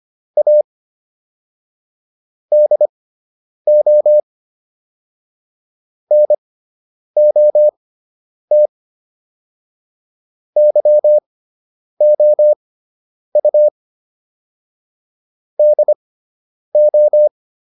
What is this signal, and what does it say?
Morse code, heard as a keyed tone, about 25 wpm, spaced as for 9 wpm, 600 Hz. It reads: A DO NOT YOU DO